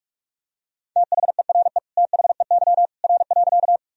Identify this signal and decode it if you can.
Morse code, heard as a keyed tone, about 46 wpm, 700 Hz. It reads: THERE THEY R.
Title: Sentences from Top 100 Words and Abbreviations 45wpm